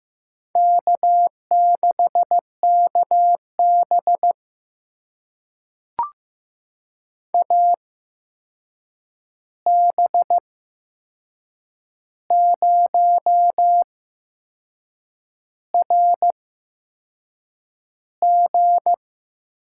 Morse code, heard as a keyed tone, about 15 wpm, spaced as for 6 wpm, 700 Hz. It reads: K6KB AB0RG
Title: USA Call Signs - 8x Character Spacing - 15wpm